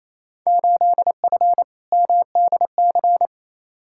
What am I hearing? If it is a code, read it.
Morse code, heard as a keyed tone, about 28 wpm, 700 Hz. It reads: 8F MDC
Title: Field Day Exchanges Encore - 28wpm